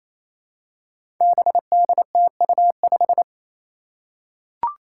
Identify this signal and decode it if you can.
Morse code, heard as a keyed tone, about 28 wpm, 700 Hz. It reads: BDTU5
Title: Five Challenging Letter-Number Combinations 28wpm